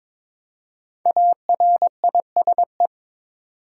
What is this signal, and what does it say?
Morse code, heard as a keyed tone, about 22 wpm, 700 Hz. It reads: ARISE
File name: 5 Letter Words - First 3 Letters Given - 22wpm